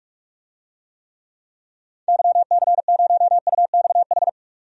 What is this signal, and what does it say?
Morse code, heard as a keyed tone, about 46 wpm, 700 Hz. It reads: YC0UXH
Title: DXCC-100 Call Signs - 6 Characters - 45wpm